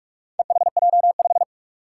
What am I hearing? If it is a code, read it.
Morse code, heard as a keyed tone, about 45 wpm, 700 Hz. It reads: EHJ5
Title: Four Challenging Letter-Number Combinations 45wpm